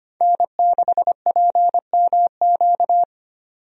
Morse code, heard as a keyed tone, about 25 wpm, 690 Hz. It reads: N6PMQ
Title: USA Call Signs - 5 Characters - 25wpm